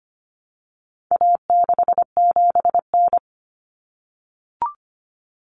Morse code, heard as a keyed tone, about 25 wpm, 700 Hz. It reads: A67N
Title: Common Cut Numbers - 4-Digit 25wpm